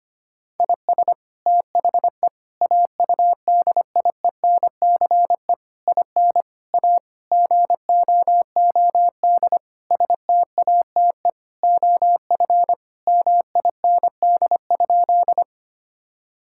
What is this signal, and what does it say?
Morse code, heard as a keyed tone, about 25 wpm, 700 Hz. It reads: IS THE AUDIENCE IN A GOOD STATE OF MIND?
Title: Sentences from Top 1000 Words - Review Part 5 - 25wpm